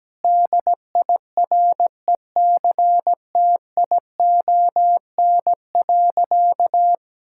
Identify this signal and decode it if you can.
Morse code, heard as a keyed tone, about 17 wpm, 700 Hz. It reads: DIRECTION.
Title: Sentences from Top 1000 Words - Review Part 5 - 17wpm